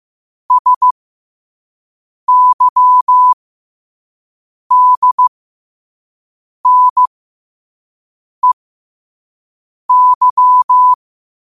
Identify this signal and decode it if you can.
Morse code, heard as a keyed tone, about 15 wpm, spaced as for 5 wpm, 1000 Hz. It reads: SYDNEY
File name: 5wpm